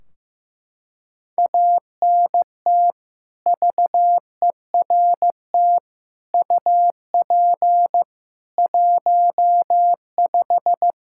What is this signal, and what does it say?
Morse code, heard as a keyed tone, about 15 wpm, 700 Hz. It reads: ANT VERT UP 15
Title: Practice QSO - Exchange 2 - Info - 15wpm